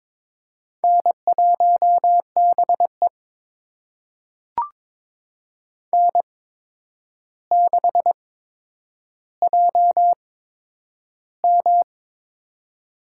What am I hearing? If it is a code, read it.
Morse code, heard as a keyed tone, about 22 wpm, spaced as for 9 wpm, 700 Hz. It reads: N1BE N6JM
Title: USA Call Signs - 4 Characters - 8x Character Spacing - 22wpm